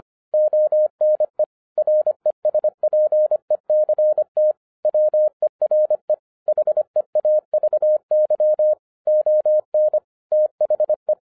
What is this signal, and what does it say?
Morse code, heard as a keyed tone, about 25 wpm, 610 Hz. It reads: ONE RESPECT WERE HEAVY ON THE
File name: 2012-01-mp3 / n14-FUN WITH NUMBERS-25wpm